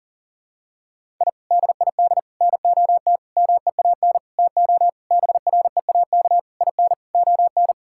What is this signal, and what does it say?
Morse code, heard as a keyed tone, about 37 wpm, 700 Hz. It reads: I DID NOT MEAN TO BREAK IN ON
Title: Sentences from Top 1000 Words - Review Part 6 - 40wpm